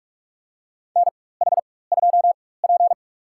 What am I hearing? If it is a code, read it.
Morse code, heard as a keyed tone, about 46 wpm, spaced as for 21 wpm, 700 Hz. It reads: NH2P
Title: USA Call Signs - 4 Characters - 4x Character Spacing - 45wpm